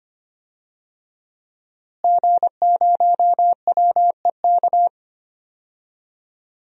Morse code, heard as a keyed tone, about 25 wpm, 700 Hz. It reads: G0WEK
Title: DXCC-100 Call Signs - 5 Characters - Repeated 2x - 25wpm